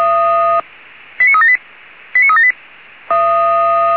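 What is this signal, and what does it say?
Five-tone selective call (ZVEI), heard as quick runs of five tones, tones of about 70 ms. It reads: 89378 89378
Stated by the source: Five-tone ZVEI selective call 89373, received by SDR.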